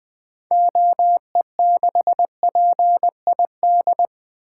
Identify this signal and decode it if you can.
Morse code, heard as a keyed tone, about 20 wpm, 700 Hz. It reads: OE6PID
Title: 2023 SOTA Activator Call Signs - 20wpm